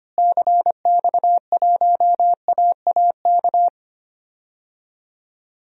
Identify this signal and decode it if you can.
Morse code, heard as a keyed tone, about 25 wpm, 700 Hz. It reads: CX1AAK
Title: DXCC-100 Call Signs - 6 Characters - Repeated 2x - 25wpm